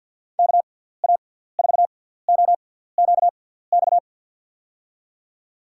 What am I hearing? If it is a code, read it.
Morse code, heard as a keyed tone, about 47 wpm, 700 Hz. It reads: K A 4 O Q X
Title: USA Call Signs - 6 Characters - 6x Character Spacing - 50wpm